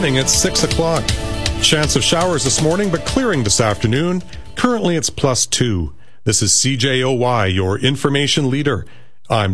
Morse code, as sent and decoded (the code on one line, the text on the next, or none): none